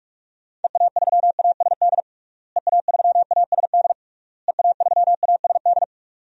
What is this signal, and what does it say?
Morse code, heard as a keyed tone, about 43 wpm, 700 Hz. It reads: EA3ASD EA3ASD EA3ASD
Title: DXCC-100 Call Signs - 6 Characters - Repeated 3x - 45wpm